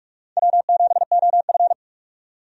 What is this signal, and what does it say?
Morse code, heard as a keyed tone, about 46 wpm, 700 Hz. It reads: W7OF
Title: USA Call Signs - 4 Characters - 45wpm